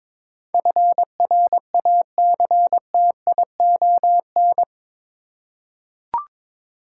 Morse code, heard as a keyed tone, about 22 wpm, 700 Hz. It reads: FRACTION
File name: practice / 22wpm / Eight Letter Words - 22wpm